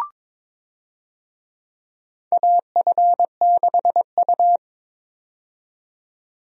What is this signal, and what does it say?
Morse code, heard as a keyed tone, about 21 wpm, 700 Hz. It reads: AF6U
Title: USA Call Signs - 4 Characters - 22wpm